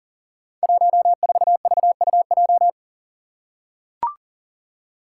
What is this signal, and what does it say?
Morse code, heard as a keyed tone, about 41 wpm, 690 Hz. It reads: 14VUJ
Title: Five Challenging Letter-Number Combinations 40wpm